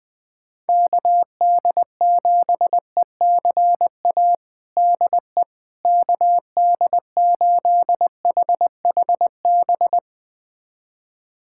Morse code, heard as a keyed tone, about 21 wpm, 700 Hz. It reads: KD7ECA DE KD8HHB
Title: USA Callsign DE Callsign - 6 Characters - 20wpm